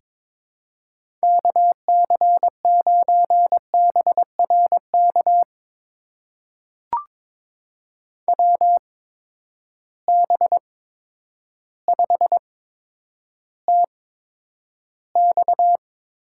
Morse code, heard as a keyed tone, about 22 wpm, spaced as for 10 wpm, 700 Hz. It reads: KC9BRK WB5TX
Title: USA Call Signs - 6 Characters - 8x Character Spacing - 22wpm